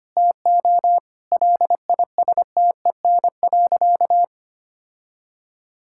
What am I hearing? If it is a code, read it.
Morse code, heard as a keyed tone, about 25 wpm, 700 Hz. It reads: TO LISTEN.
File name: practice / 25wpm / Sentences with 4-Word Phrases - 25wpm